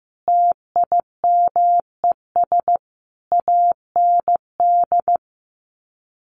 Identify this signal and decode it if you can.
Morse code, heard as a keyed tone, about 15 wpm, 700 Hz. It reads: TIMES AND
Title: Sets of 5 Words Encore 15wpm